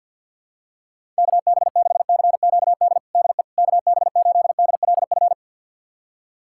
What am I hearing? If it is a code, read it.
Morse code, heard as a keyed tone, about 52 wpm, 700 Hz. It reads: KB6CQD DE KB8DLF
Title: USA Callsign DE Callsign 50wpm